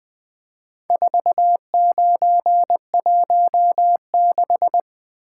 Morse code, heard as a keyed tone, about 22 wpm, 700 Hz. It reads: 4916
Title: Speed-Racing - Four Letter-Number Combinations - 20wpm